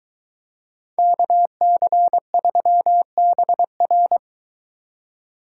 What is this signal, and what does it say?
Morse code, heard as a keyed tone, about 23 wpm, 700 Hz. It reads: KC3BR